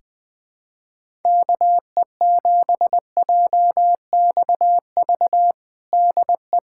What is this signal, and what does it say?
Morse code, heard as a keyed tone, about 20 wpm, 700 Hz. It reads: KE7JXV DE
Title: USA Callsign DE Callsign 20wpm